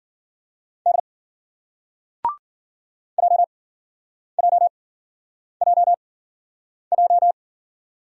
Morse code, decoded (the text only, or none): D JJJJ